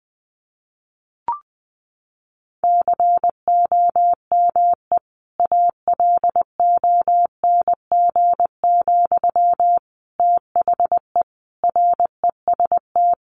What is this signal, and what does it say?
Morse code, heard as a keyed tone, about 20 wpm, 700 Hz. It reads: COME ALONG, THE REST
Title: Sentences from Top 700 Words 20wpm